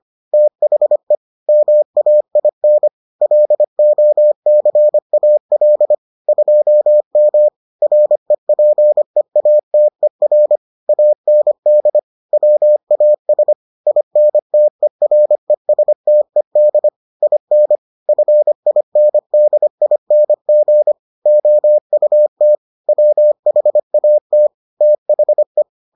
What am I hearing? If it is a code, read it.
Morse code, heard as a keyed tone, about 25 wpm, 600 Hz. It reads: THE MAIN LOCAL 2M REPEATER AND WAS INTERESTED IN FINDING OUT WHAT THE